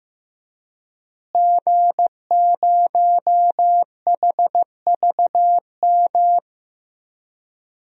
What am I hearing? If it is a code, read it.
Morse code, heard as a keyed tone, about 15 wpm, 700 Hz. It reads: G0HVM
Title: DXCC-100 Call Signs - 5 Characters - Repeated 3x - 15wpm